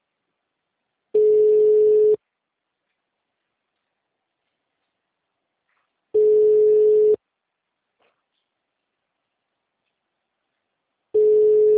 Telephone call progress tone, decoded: ringback tone

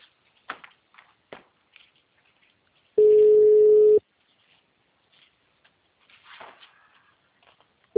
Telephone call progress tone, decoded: ringback tone

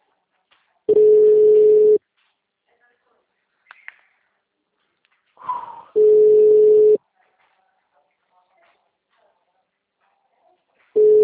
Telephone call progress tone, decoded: ringback tone

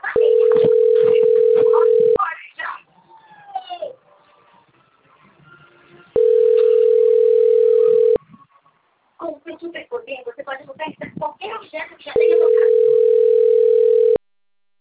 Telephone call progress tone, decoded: ringback tone

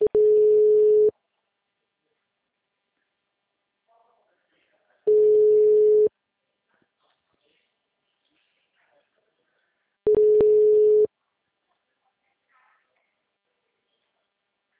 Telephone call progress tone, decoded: ringback tone

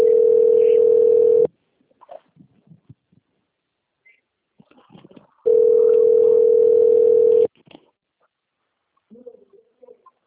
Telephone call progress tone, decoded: ringback tone

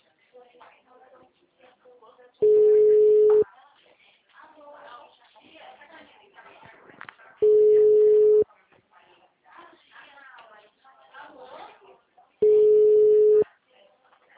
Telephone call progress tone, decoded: ringback tone

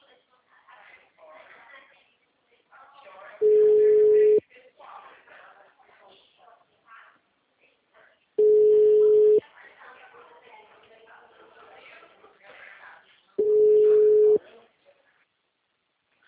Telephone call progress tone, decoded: ringback tone

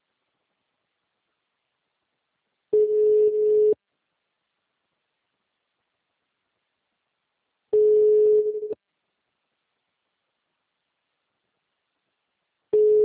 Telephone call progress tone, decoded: ringback tone

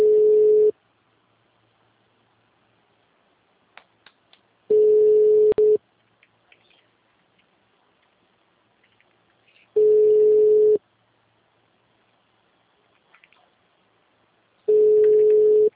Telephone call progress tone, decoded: ringback tone

